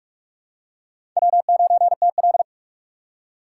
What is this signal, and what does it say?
Morse code, heard as a keyed tone, about 44 wpm, 700 Hz. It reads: W9TL
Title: USA Call Signs - 4 Characters - 3x Character Spacing - 45wpm